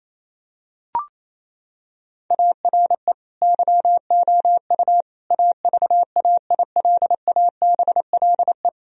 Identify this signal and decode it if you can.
Morse code, heard as a keyed tone, about 28 wpm, 700 Hz. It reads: ARE YOU AVAILABLE
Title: Sentences from Top 1000 Words - Review Part 3 - 28wpm